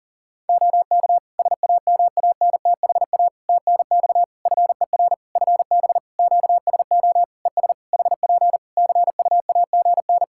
Morse code, heard as a keyed tone, about 40 wpm, 700 Hz. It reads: OK SAMANTHA TNX FER FB QSO ES HP CUAGN
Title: Practice QSO - Exchange 4 - Ending - 40wpm